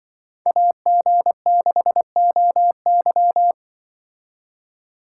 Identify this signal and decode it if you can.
Morse code, heard as a keyed tone, about 24 wpm, 700 Hz. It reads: AG6OY
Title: Speed Racing - USA 2x2 Call Signs 20wpm